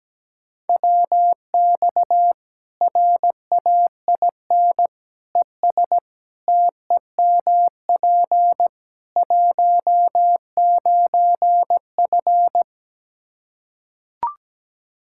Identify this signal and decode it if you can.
Morse code, heard as a keyed tone, about 17 wpm, 700 Hz. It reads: WX RAIN ES TEMP 19F
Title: Practice QSO - Exchange 2 - Info - 17wpm